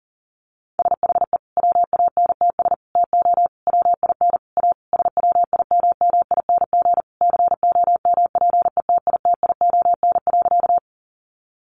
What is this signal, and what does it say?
Morse code, heard as a keyed tone, about 40 wpm, 700 Hz. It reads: SHE WANTS TO WIN A SWIMMING COMPETITION.
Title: Additional Easy Phrases 40wpm